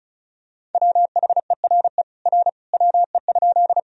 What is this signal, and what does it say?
Morse code, heard as a keyed tone, about 34 wpm, 690 Hz. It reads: WHERE R WE?
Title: Sentences from Top 200 Words and Abbreviations 35wpm